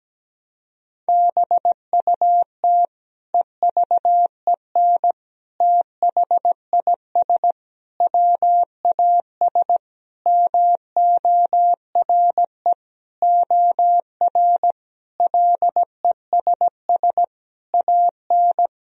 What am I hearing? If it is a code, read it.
Morse code, heard as a keyed tone, about 17 wpm, 700 Hz. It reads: BUT EVEN THIS WAS MORE OR LESS AN
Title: Sentences from Top 1000 Words - Review Part 2 - 17wpm